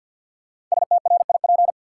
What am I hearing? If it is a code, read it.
Morse code, heard as a keyed tone, about 52 wpm, 700 Hz. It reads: STRIP
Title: Sentences from Top 1600 Words - New Word Drill Alt - 50wpm